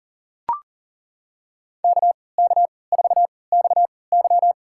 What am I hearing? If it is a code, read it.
Morse code, heard as a keyed tone, about 40 wpm, spaced as for 23 wpm, 700 Hz. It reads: KK4XY
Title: USA Call Signs - 5 Characters - 3x Character Spacing - 40wpm